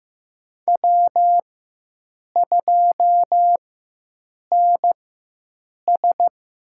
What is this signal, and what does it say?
Morse code, heard as a keyed tone, about 15 wpm, spaced as for 7 wpm, 700 Hz. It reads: W2NS
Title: USA Call Signs - 4 Characters - 4x Character Spacing - 15wpm